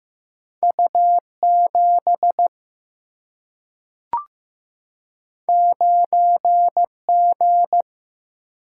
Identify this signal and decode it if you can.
Morse code, heard as a keyed tone, about 15 wpm, 700 Hz. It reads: U 7 9 G
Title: Single-Letter Single-Number 15wpm